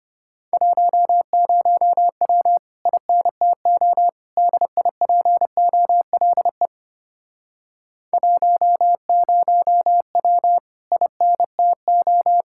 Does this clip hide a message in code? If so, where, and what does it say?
Yes: Morse code from 0.5 to 12.4 s, heard as a keyed tone, about 33 wpm, 700 Hz. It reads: 10W INTO DIPOLE 10W INTO